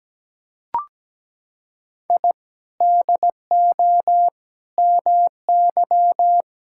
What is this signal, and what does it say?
Morse code, heard as a keyed tone, about 17 wpm, 700 Hz. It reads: I DO MY
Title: Sentences from Top 1000 Words - Review Part 6 - 17wpm